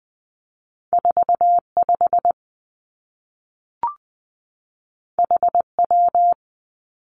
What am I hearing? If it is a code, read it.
Morse code, heard as a keyed tone, about 20 wpm, 700 Hz. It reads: 4 5 H W
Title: Two Challenging Letter-Number Combinations 20wpm